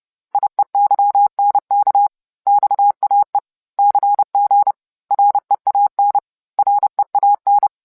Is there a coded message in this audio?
Yes: Morse code from 0.3 to 7.7 s, heard as a keyed tone, about 30 wpm, 810 Hz. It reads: IEYNK XAE CG REAN REAN